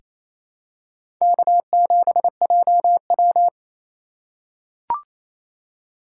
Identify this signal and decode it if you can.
Morse code, heard as a keyed tone, about 28 wpm, 700 Hz. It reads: K7JW
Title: USA Call Signs - 4 Characters - 2x Character Spacing - 28wpm